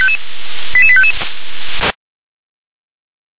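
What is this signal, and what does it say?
Five-tone selective call (ZVEI), heard as quick runs of five tones, tones of about 70 ms. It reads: 88855
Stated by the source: ZVEI five-tone selective call 88786 received by SDR.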